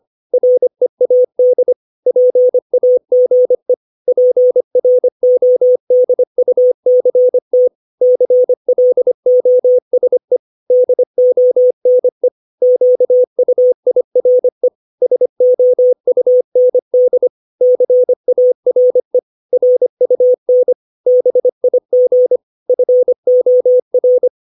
Morse code, heard as a keyed tone, about 25 wpm, 500 Hz. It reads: READ PAGE PRODUCT CLOSE DONE QUIRE SOUND CARE RUN BIG FOR
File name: 632 words / 632 Words 25WPM